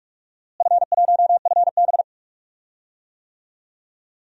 Morse code, heard as a keyed tone, about 46 wpm, 700 Hz. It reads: F1FB
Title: DXCC-100 Call Signs - 4 Characters - Repeated 3x - 45wpm